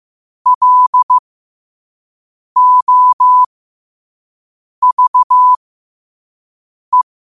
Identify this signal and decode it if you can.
Morse code, heard as a keyed tone, about 15 wpm, spaced as for 5 wpm, 1000 Hz. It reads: LOVE
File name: qotdfiles / 5wpm